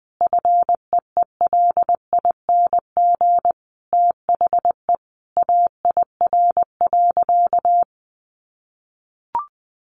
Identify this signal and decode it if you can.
Morse code, heard as a keyed tone, about 20 wpm, 700 Hz. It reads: FEELING THE AIR.
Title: A Sound Of Thunder 20wpm